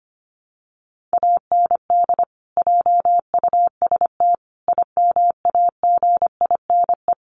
Morse code, heard as a keyed tone, about 25 wpm, 700 Hz. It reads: AND JUST IMAGINE
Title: Sentences from Top 1400 Words 25wpm